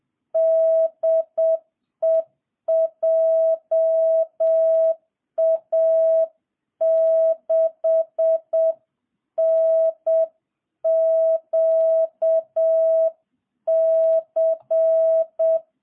Morse code, heard as a keyed tone, about 7 wpm, 660 Hz. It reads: DE JA6NQC